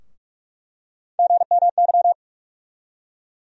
Morse code, heard as a keyed tone, about 43 wpm, 700 Hz. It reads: GMY
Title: USA Call Sign Postfixes - 3 Letters 45wpm